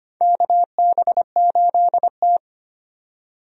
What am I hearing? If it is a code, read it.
Morse code, heard as a keyed tone, about 25 wpm, 700 Hz. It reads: KB8T
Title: USA Call Signs - 4 Characters - 25wpm